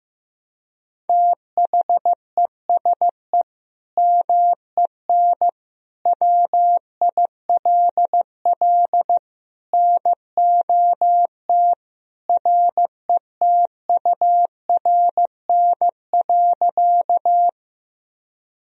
Morse code, heard as a keyed tone, about 15 wpm, 700 Hz. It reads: THESE MEN WILL NOT RETURN.
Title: Sentences from Top 1000 Words - Review Part 1 - 15wpm